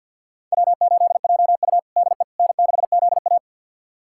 Morse code, heard as a keyed tone, about 46 wpm, 700 Hz. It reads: W8JU DE N6ZA
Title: USA Callsign DE Callsign - 4 Characters - 50wpm